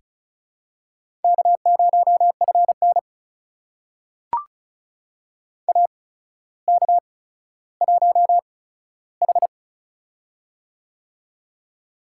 Morse code, heard as a keyed tone, about 36 wpm, spaced as for 15 wpm, 700 Hz. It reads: K0FN AK1H
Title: USA Call Signs - 4 Characters - 8x Character Spacing - 35wpm